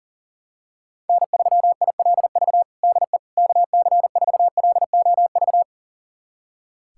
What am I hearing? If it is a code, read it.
Morse code, heard as a keyed tone, about 37 wpm, 700 Hz. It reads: N3ILV DE KC4LOV
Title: USA Callsign DE Callsign 40wpm